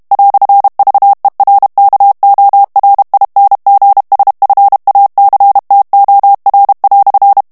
Morse code, heard as a keyed tone, about 32 wpm, 800 Hz. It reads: "VERKORINGSFACTOR"
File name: morse mp3 / HOLLANDAIS texte clair 1 vitesse 30 mots minute